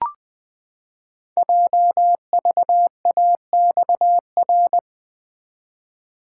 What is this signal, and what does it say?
Morse code, heard as a keyed tone, about 20 wpm, 700 Hz. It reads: JVAXR
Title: Code Groups - 5 Letter:Numbers - 20wpm